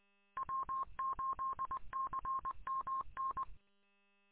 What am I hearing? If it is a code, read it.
Morse code, heard as a keyed tone, about 23 wpm, 1060 Hz. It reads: W8CMN
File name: w8cmn- -combined